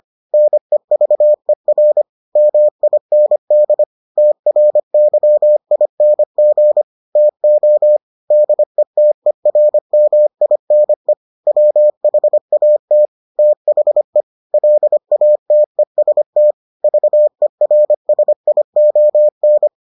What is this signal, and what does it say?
Morse code, heard as a keyed tone, about 25 wpm, 600 Hz. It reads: NEVER MIND TRYING TO DETERMINE WHAT THE LATEST VERSION